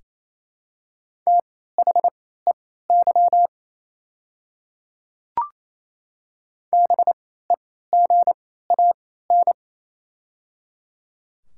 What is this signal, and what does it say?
Morse code, heard as a keyed tone, about 28 wpm, spaced as for 16 wpm, 700 Hz. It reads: THEY BEGAN
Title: 3x Character Spacing - Top 300 Words - 28wpm